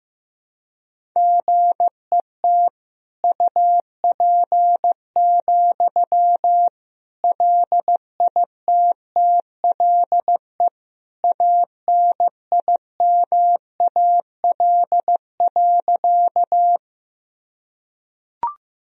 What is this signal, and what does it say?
Morse code, heard as a keyed tone, about 15 wpm, 700 Hz. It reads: GET UP, LITTLE ANIMAL.
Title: Sentences from Top 1000 Words - Review Part 1 - 15wpm